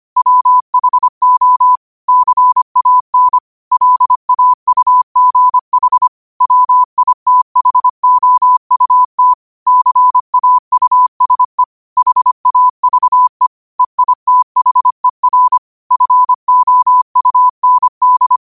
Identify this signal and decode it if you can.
Morse code, heard as a keyed tone, about 25 wpm, 1000 Hz. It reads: WHO CAN LAUGH WITHOUT CAUSE HAVE EITHER FOUND
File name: qotdfiles / 25wpm